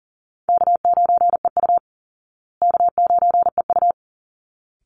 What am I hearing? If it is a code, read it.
Morse code, heard as a keyed tone, about 40 wpm, 700 Hz. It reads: K9EU K9EU
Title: Easy Call Signs Repeated 2x - 40wpm